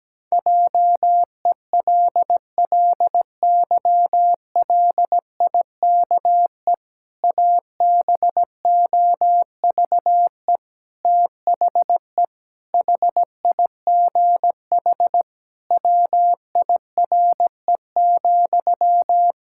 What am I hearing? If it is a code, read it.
Morse code, heard as a keyed tone, about 17 wpm, 700 Hz. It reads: JELLYLIKE ABOVE THE HIGH WIRE,
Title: Pangram Challenge v2 - 17wpm